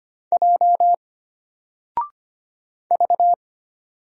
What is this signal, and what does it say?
Morse code, heard as a keyed tone, about 25 wpm, 700 Hz. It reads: J V
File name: Single-Letter-Number-25wpm